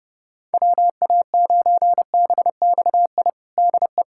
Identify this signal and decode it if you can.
Morse code, heard as a keyed tone, about 32 wpm, 700 Hz. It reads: WA9BXI DE
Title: Practice QSO - Exchange 4 - Ending - 30wpm